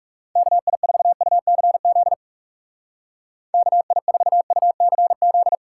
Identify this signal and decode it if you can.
Morse code, heard as a keyed tone, about 37 wpm, 700 Hz. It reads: KI4UCZ KI4UCZ
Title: Speed Racing - USA 2x3 Call Signs 30wpm